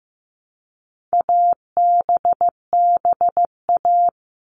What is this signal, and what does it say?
Morse code, heard as a keyed tone, about 15 wpm, 700 Hz. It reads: ABBA
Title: All Cut Numbers - 4-Digit 15wpm